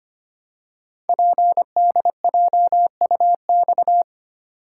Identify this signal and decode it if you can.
Morse code, heard as a keyed tone, about 25 wpm, 700 Hz. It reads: PDJUX
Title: Code Groups - 5 Letters - 25wpm